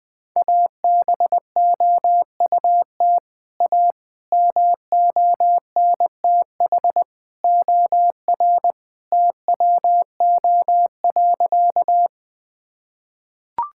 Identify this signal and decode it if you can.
Morse code, heard as a keyed tone, about 20 wpm, 700 Hz. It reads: ABOUT A MONTH OR TWO.
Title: Sentences from Top 1000 Words - Review Part 5 - 20wpm